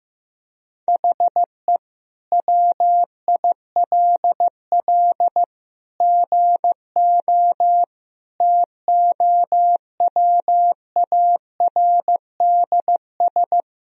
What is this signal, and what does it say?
Morse code, heard as a keyed tone, about 15 wpm, 700 Hz. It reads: HE WILL GO TOWARDS
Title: Sentences from Top 1000 Words - Review Part 2 - 15wpm